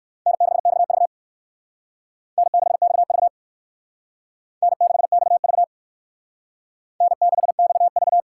Morse code, heard as a keyed tone, about 62 wpm, 700 Hz. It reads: N6XV N6XV N6XV N6XV